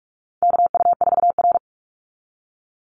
Morse code, heard as a keyed tone, about 44 wpm, 700 Hz. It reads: KU4R
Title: NA SOTA Chaser Callsigns 45wpm